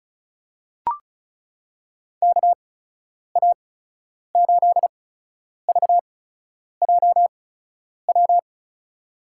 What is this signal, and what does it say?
Morse code, heard as a keyed tone, about 34 wpm, spaced as for 10 wpm, 700 Hz. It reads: KA8VJW
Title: USA Call Signs - 6 Characters - 8x Character Spacing - 35wpm